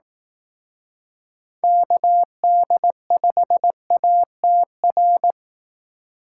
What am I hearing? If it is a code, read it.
Morse code, heard as a keyed tone, about 18 wpm, 700 Hz. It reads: KD5ATR
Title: Speed Racing - USA 2x3 Call Signs 15wpm